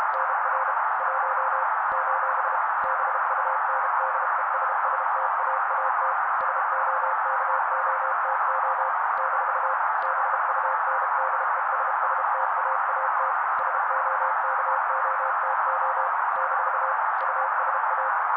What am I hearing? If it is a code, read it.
Morse code, heard as a keyed tone, about 32 wpm, 530 Hz, with noise. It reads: NR 1 : BUNDESSTAAT SOKOTO - BUNDESSTAAT SOKOTO - USA
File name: daily / 31wpm noise